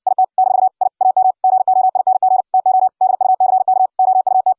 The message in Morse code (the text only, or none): IT - I AM GOING TO DUCK OUT